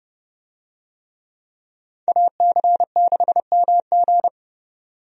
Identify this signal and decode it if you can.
Morse code, heard as a keyed tone, about 30 wpm, 700 Hz. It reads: AC6MG
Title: USA Call Signs - 5 Characters - 30wpm